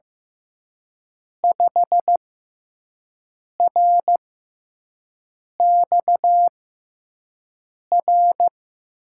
Morse code, heard as a keyed tone, about 15 wpm, spaced as for 5 wpm, 700 Hz. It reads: 5RXR
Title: USA Call Signs - 6x Character Spacing - 15wpm